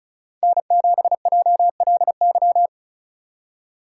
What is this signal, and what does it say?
Morse code, heard as a keyed tone, about 36 wpm, 690 Hz. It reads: N7JLY
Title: USA Call Signs - 3x Character Spacing - 35wpm